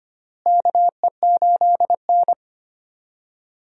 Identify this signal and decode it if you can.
Morse code, heard as a keyed tone, about 25 wpm, 700 Hz. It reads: KE8N
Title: USA Call Signs - 4 Characters - Repeated 3x - 25wpm